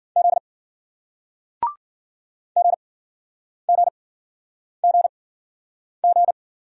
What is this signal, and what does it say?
Morse code, heard as a keyed tone, about 56 wpm, spaced as for 9 wpm, 700 Hz. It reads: Z GGGG